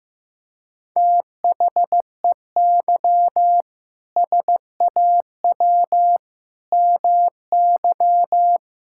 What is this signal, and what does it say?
Morse code, heard as a keyed tone, about 15 wpm, 700 Hz. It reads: THEY SAW MY
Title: Sentences from Top 1600 Words 15wpm